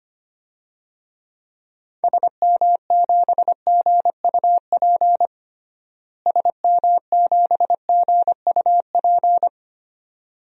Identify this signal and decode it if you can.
Morse code, heard as a keyed tone, about 25 wpm, 700 Hz. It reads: SM7GUP SM7GUP
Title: DXCC-100 Call Signs - 6 Characters - Repeated 2x - 25wpm